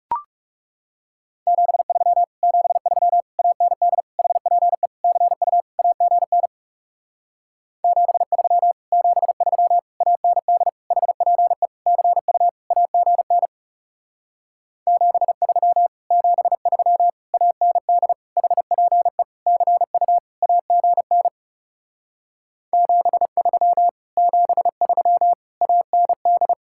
Morse code, heard as a keyed tone, about 39 wpm, 700 Hz. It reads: 73 73 AND HPE CU AGN 73 73 AND HPE CU AGN 73 73 AND HPE CU AGN 73 73 AND